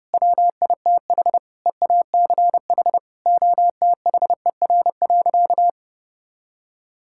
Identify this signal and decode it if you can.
Morse code, heard as a keyed tone, about 30 wpm, 700 Hz. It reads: WITH EACH OTHER.
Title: Sentences from Top 1000 Words - Review Part 6 - 30wpm